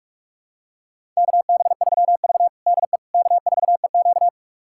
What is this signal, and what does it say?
Morse code, heard as a keyed tone, about 47 wpm, 700 Hz. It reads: KB3V DE K4EQ
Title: USA Callsign DE Callsign - 4 Characters - 45wpm